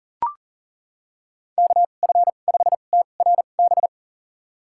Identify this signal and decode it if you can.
Morse code, heard as a keyed tone, about 40 wpm, spaced as for 29 wpm, 700 Hz. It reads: KF5TRB
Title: USA Call Signs - 6 Characters - 2x Character Spacing - 40wpm